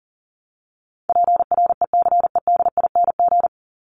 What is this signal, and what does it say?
Morse code, heard as a keyed tone, about 40 wpm, 700 Hz. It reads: PRECEDING
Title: ING Suffix - Top 100 Words with Base Words 40wpm